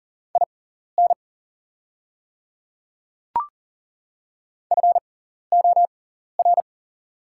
Morse code, heard as a keyed tone, about 40 wpm, spaced as for 14 wpm, 690 Hz. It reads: IN FOR
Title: 6x Character Spacing - Top 200 Words - 40wpm